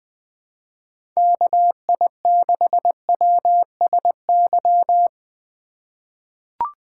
Morse code, heard as a keyed tone, about 20 wpm, 700 Hz. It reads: KI6WSY
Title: USA Call Signs - 6 Characters - 4x Character Spacing - 20wpm